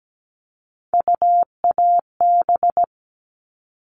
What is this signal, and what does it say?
Morse code, heard as a keyed tone, about 17 wpm, 700 Hz. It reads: UAB
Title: All Cut Numbers - 3-Digit 17wpm